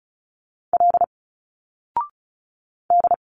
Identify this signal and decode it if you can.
Morse code, heard as a keyed tone, about 36 wpm, 700 Hz. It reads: L D
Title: Single Letters 35wpm